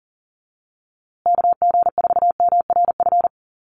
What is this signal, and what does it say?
Morse code, heard as a keyed tone, about 40 wpm, 700 Hz. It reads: KG4MRF
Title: USA Call Signs Repeated 2x 40wpm